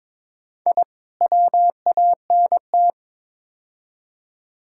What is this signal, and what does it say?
Morse code, heard as a keyed tone, about 21 wpm, 700 Hz. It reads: I WANT
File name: practice / 22wpm / Build A Phrase - 5-Word Phrases - 22wpm